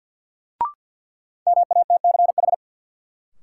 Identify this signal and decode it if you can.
Morse code, heard as a keyed tone, about 52 wpm, 700 Hz. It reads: MATCH
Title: Sentences from Top 1600 Words - New Word Drill - 50wpm